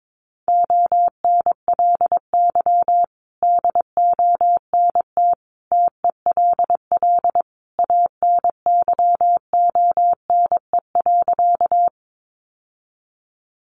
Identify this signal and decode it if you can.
Morse code, heard as a keyed tone, about 22 wpm, 700 Hz. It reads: ONLY DONT TELL ANYONE.
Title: Sentences from Top 900 Words 22wpm